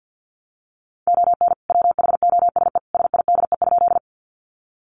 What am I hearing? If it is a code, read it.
Morse code, heard as a keyed tone, about 50 wpm, 700 Hz. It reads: ON WHOSE SIDE?
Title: Sentences from Top 1200 Words 50wpm